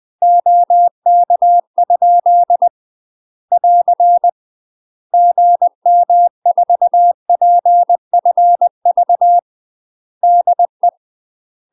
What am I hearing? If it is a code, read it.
Morse code, heard as a keyed tone, about 20 wpm, 700 Hz. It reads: OK? <AR> GM4PFV DE
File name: UK MASTER QSOs TRAINING / 20WPM / UK MASTER QSOs 0024